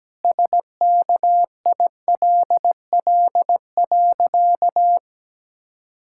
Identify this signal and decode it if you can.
Morse code, heard as a keyed tone, about 17 wpm, 700 Hz. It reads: SKILL.
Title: Pangram Challenge v2 - 17wpm